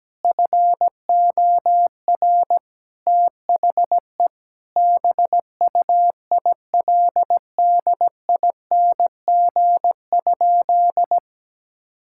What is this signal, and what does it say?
Morse code, heard as a keyed tone, about 17 wpm, 700 Hz. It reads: FOR THE BUILDING?
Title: Sentences from Top 1000 Words - Review Part 5 - 17wpm